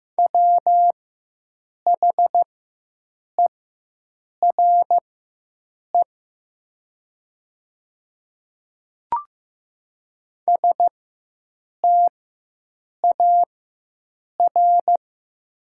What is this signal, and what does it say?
Morse code, heard as a keyed tone, about 15 wpm, spaced as for 7 wpm, 700 Hz. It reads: WHERE STAR
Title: 4x Character Spacing - Top 500 Words - 15wpm